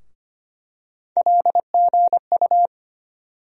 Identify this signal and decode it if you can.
Morse code, heard as a keyed tone, about 25 wpm, 700 Hz. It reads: LGU
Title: USA Call Sign Postfixes - 3 Letters 25wpm